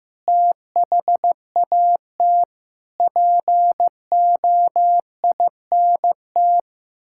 Morse code, heard as a keyed tone, about 16 wpm, 700 Hz. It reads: THAT POINT
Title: Sentences from Top 1000 Words - Review Part 5 - 15wpm